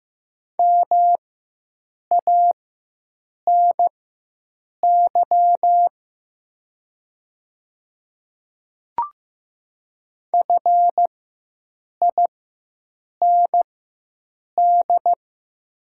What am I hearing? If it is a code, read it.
Morse code, heard as a keyed tone, about 15 wpm, spaced as for 7 wpm, 700 Hz. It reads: MANY FIND
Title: 4x Character Spacing - Top 100 Words - 15wpm